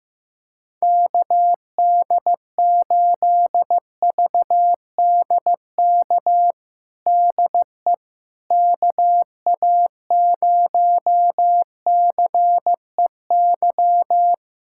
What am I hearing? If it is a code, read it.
Morse code, heard as a keyed tone, about 15 wpm, 700 Hz. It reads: KD8VDK DE KA0CEY
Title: USA Callsign DE Callsign 15wpm